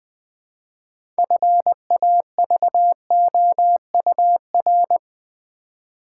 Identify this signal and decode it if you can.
Morse code, heard as a keyed tone, about 20 wpm, 700 Hz. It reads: FAVOUR
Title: Build A Word - 3 Letters At A Time - 6-Letter Words - 20wpm